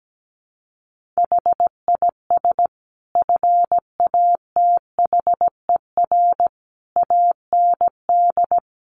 Morse code, heard as a keyed tone, about 17 wpm, 700 Hz. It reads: HIS FATHER AND
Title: Sentences from Top 1500 Words 17wpm